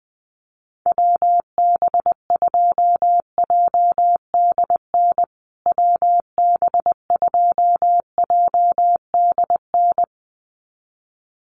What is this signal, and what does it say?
Morse code, heard as a keyed tone, about 20 wpm, 700 Hz. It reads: WB2JDN WB2JDN
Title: USA Call Signs Repeated 2x 20wpm